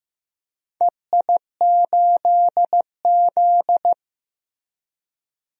Morse code, heard as a keyed tone, about 16 wpm, 700 Hz. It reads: EI8Z
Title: DXCC-100 Call Signs - 4 Characters - Repeated 3x - 15wpm